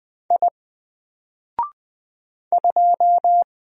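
Morse code, heard as a keyed tone, about 20 wpm, 700 Hz. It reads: I 2
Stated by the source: Single letter and number Morse drill at 20 wpm, from N4LCD.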